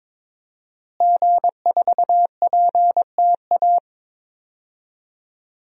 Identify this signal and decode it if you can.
Morse code, heard as a keyed tone, about 22 wpm, 700 Hz. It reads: G4PTA
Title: DXCC-100 Call Signs - 5 Characters - Repeated 3x - 22wpm